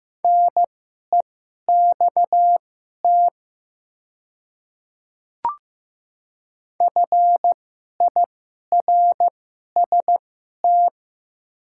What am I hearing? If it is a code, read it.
Morse code, heard as a keyed tone, about 15 wpm, spaced as for 11 wpm, 700 Hz. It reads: NEXT FIRST
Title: 2x Character Spacing - Top 300 Words - 15wpm